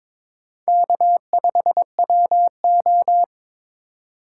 Morse code, heard as a keyed tone, about 22 wpm, 700 Hz. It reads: K5WO
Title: USA Call Signs - 4 Characters - 2x Character Spacing - 22wpm